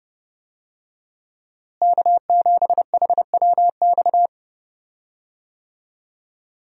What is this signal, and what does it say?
Morse code, heard as a keyed tone, about 30 wpm, 700 Hz. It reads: K7HWX
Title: USA Call Signs - 5 Characters - Repeated 3x - 30wpm